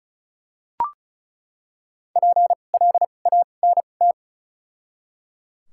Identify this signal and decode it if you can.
Morse code, heard as a keyed tone, about 34 wpm, 700 Hz. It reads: P L A N T
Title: 2x Character Spacing - Top 300 Words - 35wpm